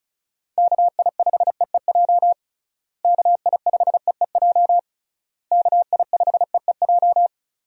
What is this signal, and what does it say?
Morse code, heard as a keyed tone, about 36 wpm, 700 Hz. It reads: KI5EEJ KI5EEJ KI5EEJ
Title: USA Call Signs - 6 Characters - Repeated 3x - 35wpm